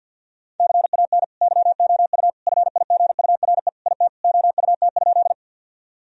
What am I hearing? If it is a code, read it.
Morse code, heard as a keyed tone, about 52 wpm, 700 Hz. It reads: CAN YOU FIGURE IT OUT?
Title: Sentences from Top 1000 Words - Review Part 2 - 50wpm